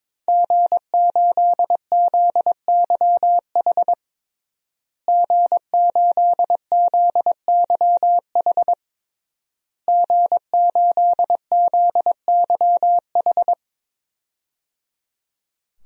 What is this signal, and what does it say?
Morse code, heard as a keyed tone, about 22 wpm, 700 Hz. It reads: G8ZYH G8ZYH G8ZYH
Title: DXCC-100 Call Signs - 5 Characters - Repeated 3x - 22wpm